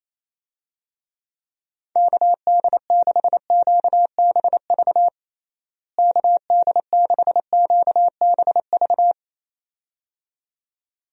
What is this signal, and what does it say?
Morse code, heard as a keyed tone, about 27 wpm, 700 Hz. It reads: KD6QBV KD6QBV
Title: USA Call Signs - 6 Characters - Repeated 2x - 28wpm